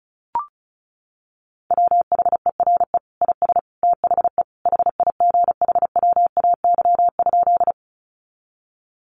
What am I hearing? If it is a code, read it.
Morse code, heard as a keyed tone, about 34 wpm, 700 Hz. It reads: WHERE IS THE HIGHWAY?
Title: Sentences from Top 1400 Words 35wpm